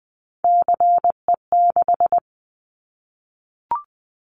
Morse code, heard as a keyed tone, about 20 wpm, 700 Hz. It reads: CE6
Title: Callsign Prefixes for Top 100 DXCC - 20wpm